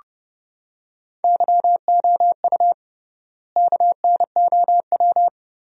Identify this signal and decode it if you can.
Morse code, heard as a keyed tone, about 30 wpm, 700 Hz. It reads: YOU KNOW
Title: 3x Word-Spacing - Sentences from Top 100 Words - 30wpm